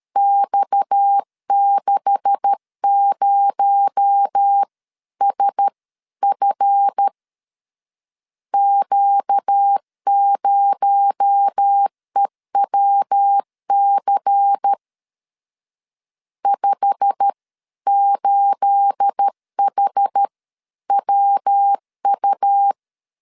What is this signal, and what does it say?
Morse code, heard as a keyed tone, about 13 wpm, spaced as for 11 wpm, 800 Hz. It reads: X60SF Q0EWC 58HWU
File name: CW / DET / 1 9 1500C S B 12wpm 3819